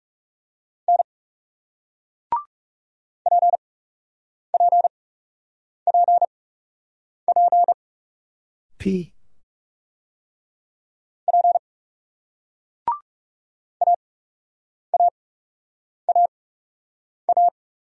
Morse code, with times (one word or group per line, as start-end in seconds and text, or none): 0.9-1.0 N
3.3-7.7 PPPP
11.3-11.6 P
13.8-17.5 AAAA